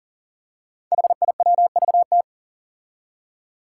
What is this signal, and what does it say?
Morse code, heard as a keyed tone, about 40 wpm, 700 Hz. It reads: HIWVT
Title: Five Challenging Letter-Number Combinations 40wpm